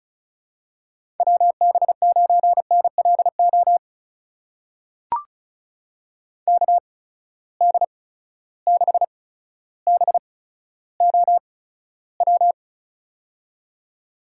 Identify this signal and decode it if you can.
Morse code, heard as a keyed tone, about 34 wpm, spaced as for 15 wpm, 700 Hz. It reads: WB9NLO KD6BOW
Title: USA Call Signs - 8x Character Spacing - 35wpm